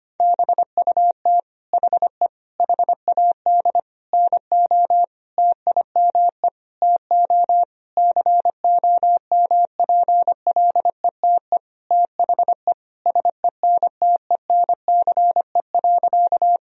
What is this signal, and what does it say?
Morse code, heard as a keyed tone, about 25 wpm, 700 Hz. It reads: BUT HE HAD NO TIME TO COMPLETE THE SENTENCE.
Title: Sentences from Top 1000 Words - Review Part 1 - 25wpm